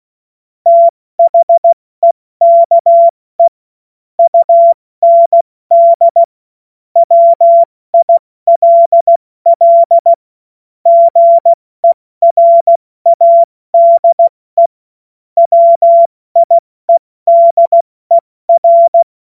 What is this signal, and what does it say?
Morse code, heard as a keyed tone, about 16 wpm, 680 Hz. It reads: THEKE UND WILL GERADE WIEDER
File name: Morsekurs / Witze / Witz 15. November 2022 70 F80